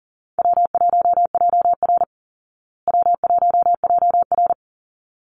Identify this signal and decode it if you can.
Morse code, heard as a keyed tone, about 38 wpm, 700 Hz. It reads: W1JR W1JR
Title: Easy Call Signs Repeated 2x - 40wpm